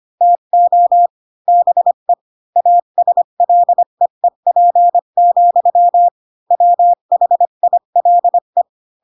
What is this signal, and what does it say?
Morse code, heard as a keyed tone, about 25 wpm, 700 Hz. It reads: TO BE ASLEEP, WHILE
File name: Pinocchio / Pinocchio ch8 20 25 ik1hge